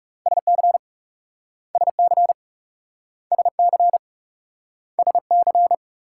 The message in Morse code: SC SC SC SC

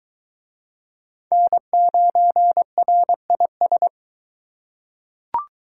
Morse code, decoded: N9RIS

